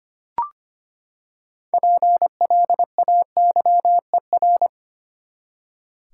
Morse code, heard as a keyed tone, about 25 wpm, 700 Hz. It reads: PLAYER
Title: Six Letter Words - 25wpm